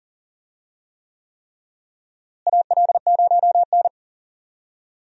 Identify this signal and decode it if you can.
Morse code, heard as a keyed tone, about 40 wpm, 700 Hz. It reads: AL0N
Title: USA Call Signs - 4 Characters - Repeated 3x - 40wpm